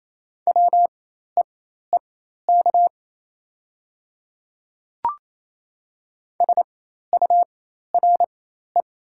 Morse code, heard as a keyed tone, about 28 wpm, spaced as for 13 wpm, 700 Hz. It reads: WEEK SURE
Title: 4x Character Spacing - Top 500 Words - 28wpm